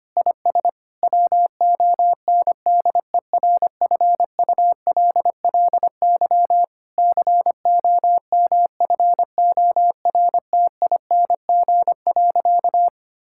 Morse code, heard as a keyed tone, about 25 wpm, 700 Hz. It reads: IS WONDERFULLY COMFORTING.
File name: practice / 25wpm / Difficult Sentences with Hints - 25wpm